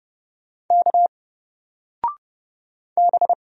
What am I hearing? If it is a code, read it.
Morse code, heard as a keyed tone, about 30 wpm, 700 Hz. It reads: K B
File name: Single-Letter-Number-30wpm